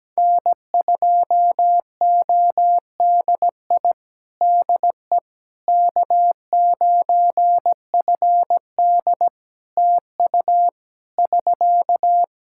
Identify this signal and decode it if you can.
Morse code, heard as a keyed tone, about 17 wpm, 700 Hz. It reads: N2ODI DE K9FD TU <SK>
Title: Practice QSO - Exchange 4 - Ending - 17wpm